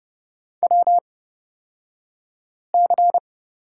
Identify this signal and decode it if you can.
Morse code, heard as a keyed tone, about 30 wpm, spaced as for 5 wpm, 700 Hz. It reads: WC
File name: Lesson 37d - Letter x - Two Characters 30wpm